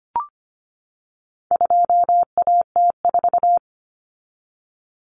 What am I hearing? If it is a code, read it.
Morse code, heard as a keyed tone, about 25 wpm, 700 Hz. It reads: 2AT4
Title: Common Cut Numbers - 4-Digit 25wpm